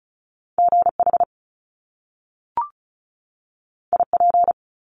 Morse code, heard as a keyed tone, about 36 wpm, 700 Hz. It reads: GH IP
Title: Top Two Letter Combinations in Words 35wpm